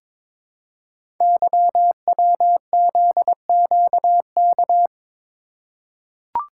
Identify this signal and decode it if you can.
Morse code, heard as a keyed tone, about 22 wpm, 700 Hz. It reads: YWZQK
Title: Code Groups - 5 Letters - 22wpm